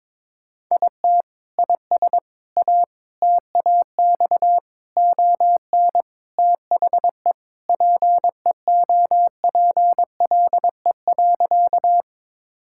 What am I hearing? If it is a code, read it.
Morse code, heard as a keyed tone, about 22 wpm, 700 Hz. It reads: IT IS A TAX ON THE PEOPLE.
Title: Sentences from Top 1000 Words - Review Part 5 - 22wpm